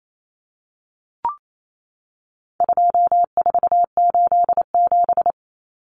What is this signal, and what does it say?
Morse code, heard as a keyed tone, about 28 wpm, 700 Hz. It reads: 2487
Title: Numbers - 4-Digit 28wpm